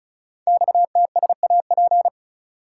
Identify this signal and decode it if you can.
Morse code, heard as a keyed tone, about 36 wpm, 700 Hz. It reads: XTSAP
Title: Code Groups - 5 Letters - 35wpm